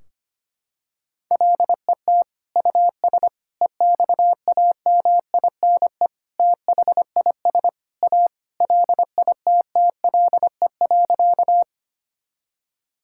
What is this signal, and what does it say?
Morse code, heard as a keyed tone, about 25 wpm, 700 Hz. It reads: LET US EXAMINE THIS A LITTLE.
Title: Sentences from Top 1600 Words 25wpm